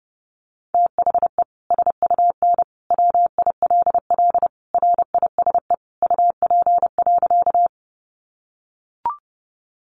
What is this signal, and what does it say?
Morse code, heard as a keyed tone, about 30 wpm, 700 Hz. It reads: THE SUN WILL RISE UP.
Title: Sentences from Top 900 Words 30wpm